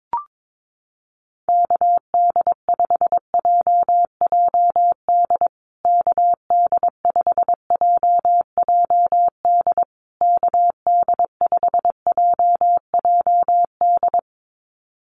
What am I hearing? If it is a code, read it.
Morse code, heard as a keyed tone, about 22 wpm, 700 Hz. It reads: KD5JJD KD5JJD KD5JJD
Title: USA Call Signs Repeated 3x 22wpm